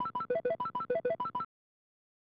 There is no code.